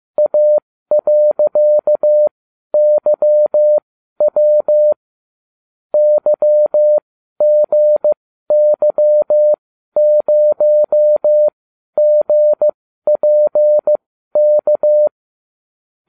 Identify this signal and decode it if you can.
Morse code, heard as a keyed tone, about 15 wpm, spaced as for 12 wpm, 600 Hz. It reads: A.YW YGY0GPK